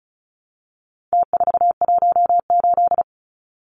Morse code, heard as a keyed tone, about 34 wpm, 700 Hz. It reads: T418
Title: Numbers - Contest Style - 4-Digit 35wpm